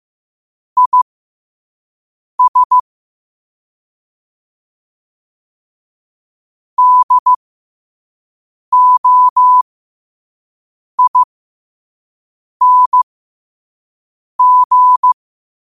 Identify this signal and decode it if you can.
Morse code, heard as a keyed tone, about 15 wpm, spaced as for 5 wpm, 1000 Hz. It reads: IS DOING